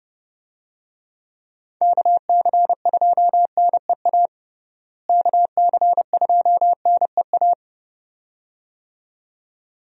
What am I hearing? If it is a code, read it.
Morse code, heard as a keyed tone, about 30 wpm, 700 Hz. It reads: KC2NEA KC2NEA
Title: USA Call Signs - 6 Characters - Repeated 2x - 30wpm